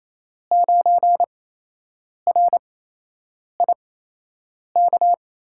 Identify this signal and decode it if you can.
Morse code, heard as a keyed tone, about 28 wpm, spaced as for 8 wpm, 700 Hz. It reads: 9RIK
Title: USA Call Signs - 5 Characters - 8x Character Spacing - 28wpm